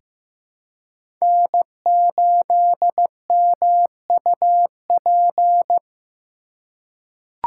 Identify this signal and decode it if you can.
Morse code, heard as a keyed tone, about 15 wpm, 700 Hz. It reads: N8MUP
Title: USA Call Signs - 5 Characters - 4x Character Spacing - 15wpm